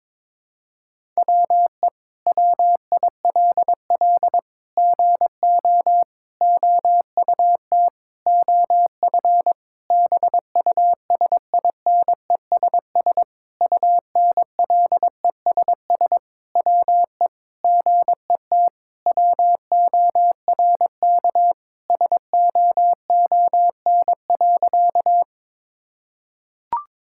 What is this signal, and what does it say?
Morse code, heard as a keyed tone, about 22 wpm, 700 Hz. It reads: WE WILL GO OUT OF BUSINESS UNLESS WE GET WORK SOON.
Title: Sentences from Top 1000 Words - Review Part 6 - 22wpm